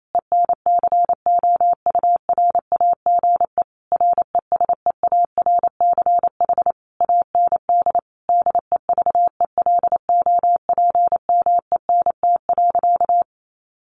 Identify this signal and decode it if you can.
Morse code, heard as a keyed tone, about 28 wpm, 700 Hz. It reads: ENCOURAGE RESEARCH AND DEVELOPMENT.
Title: Sentences from Top 1400 Words 28wpm